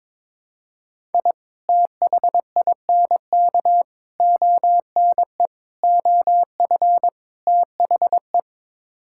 Morse code, heard as a keyed tone, about 21 wpm, 700 Hz. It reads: I THINK ONE OF THE
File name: practice / 22wpm / Build A Phrase - 5-Word Phrases - 22wpm